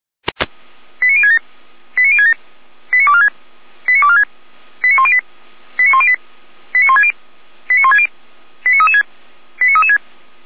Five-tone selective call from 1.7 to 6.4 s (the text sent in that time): 89067 89246 89246 89108 89108